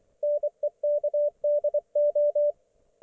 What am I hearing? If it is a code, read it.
Morse code, heard as a keyed tone, about 24 wpm, 590 Hz. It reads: NEKDO
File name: Archiv / CUC212 / ok1fhi cul212